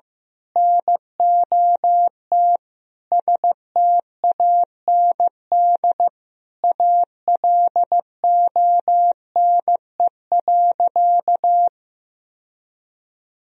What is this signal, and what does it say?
Morse code, heard as a keyed tone, about 15 wpm, 700 Hz. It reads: NOT STAND ALONE.
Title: Sentences from Top 1000 Words - Review Part 3 - 15wpm